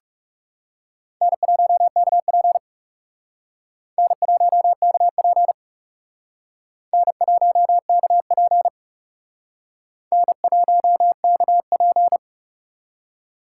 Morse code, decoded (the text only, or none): N1KP N1KP N1KP N1KP